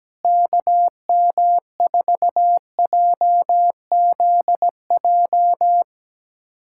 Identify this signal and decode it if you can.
Morse code, heard as a keyed tone, about 17 wpm, 700 Hz. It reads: KM4JZJ
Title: USA Callsign DE Callsign - 6 Characters - 17wpm